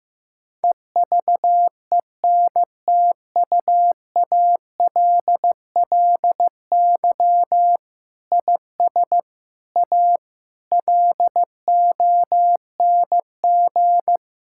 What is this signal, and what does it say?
Morse code, heard as a keyed tone, about 15 wpm, 700 Hz. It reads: EVENTUALLY IS A LONG
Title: Sentences from Top 1000 Words - Review Part 6 - 15wpm